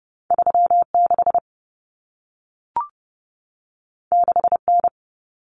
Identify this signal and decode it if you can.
Morse code, heard as a keyed tone, about 30 wpm, 700 Hz. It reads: 36 6N
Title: Common Cut Numbers - 2-Digit 30wpm